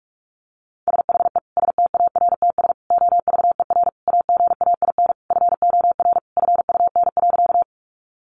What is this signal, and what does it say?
Morse code, heard as a keyed tone, about 44 wpm, 700 Hz. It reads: SHE STARTS OVER AGAIN FOR FUN.
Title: Easy Phrases 45wpm